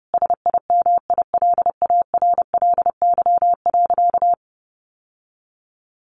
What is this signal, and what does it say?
Morse code, heard as a keyed tone, about 30 wpm, 700 Hz. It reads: SIMILARLY.
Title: Sentences from Top 1400 Words 30wpm